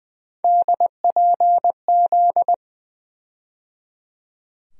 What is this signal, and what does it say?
Morse code, heard as a keyed tone, about 20 wpm, 700 Hz. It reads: DPZ
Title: USA Call Sign Postfixes - 3 Letters 20wpm